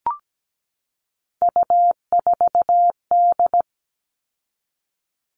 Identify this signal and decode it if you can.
Morse code, heard as a keyed tone, about 17 wpm, 700 Hz. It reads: U4D
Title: All Cut Numbers - 3-Digit 17wpm